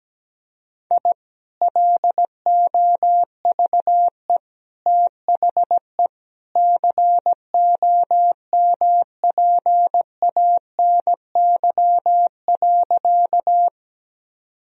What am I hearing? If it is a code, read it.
Morse code, heard as a keyed tone, about 17 wpm, 700 Hz. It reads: I LOVE THE COMPANY.
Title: Sentences from Top 1000 Words - Review Part 3 - 17wpm